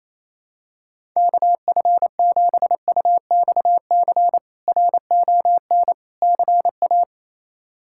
Morse code, heard as a keyed tone, about 28 wpm, 700 Hz. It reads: KF7UXC RON CA
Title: K1USN SST QSO Practice - 28wpm